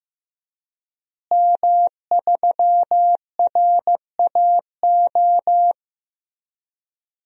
Morse code, heard as a keyed tone, about 15 wpm, 700 Hz. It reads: M3RAO
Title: DXCC-100 Call Signs - 5 Characters - Repeated 3x - 15wpm